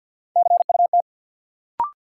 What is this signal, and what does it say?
Morse code, heard as a keyed tone, about 53 wpm, 700 Hz. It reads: CUT
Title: Sentences from Top 300 Words - New Word Drill - 50wpm